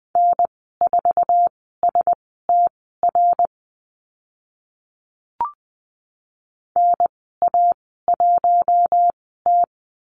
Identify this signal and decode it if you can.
Morse code, heard as a keyed tone, about 20 wpm, spaced as for 14 wpm, 700 Hz. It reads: N4STR NA1T
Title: 069 Lesson 12g - Letter d - Call Signs 20wpm